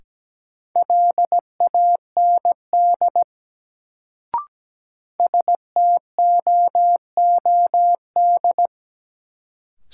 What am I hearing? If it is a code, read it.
Morse code, heard as a keyed tone, about 17 wpm, 700 Hz. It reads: LAND STOOD
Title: Sentences from Top 500 Words - New Word Drill - 17wpm